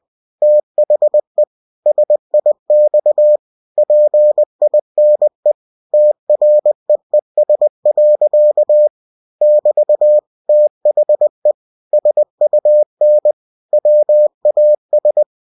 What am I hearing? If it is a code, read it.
Morse code, heard as a keyed tone, about 20 wpm, 600 Hz. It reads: THE SIX PINE TREES. <BT> THE SUN WAS
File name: cw-books / winnie-the-pooh / winnie-20wpm-page-0040